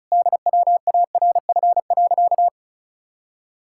Morse code, heard as a keyed tone, about 36 wpm, 700 Hz. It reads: DWARF.
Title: Pangram Challenge v2 - 35wpm